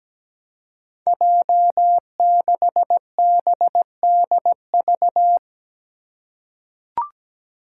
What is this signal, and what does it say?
Morse code, heard as a keyed tone, about 17 wpm, 700 Hz. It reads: J6BDV
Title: Five Challenging Letter-Number Combinations 17wpm